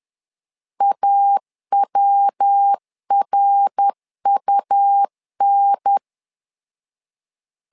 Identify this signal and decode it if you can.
Morse code, heard as a keyed tone, about 10 wpm, 800 Hz. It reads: AWRUN